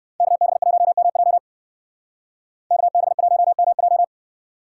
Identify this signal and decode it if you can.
Morse code, heard as a keyed tone, about 63 wpm, 700 Hz. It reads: KB1MJ KB1MJ